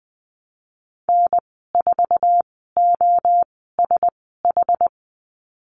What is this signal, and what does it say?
Morse code, heard as a keyed tone, about 20 wpm, spaced as for 15 wpm, 700 Hz. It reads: N4OSH
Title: 062 Lesson 11g - Letter h - Call Signs 20wpm